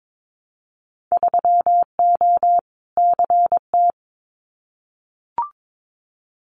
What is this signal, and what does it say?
Morse code, heard as a keyed tone, about 22 wpm, 700 Hz. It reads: 3O CT
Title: Winter Field Day Exchanges 22wpm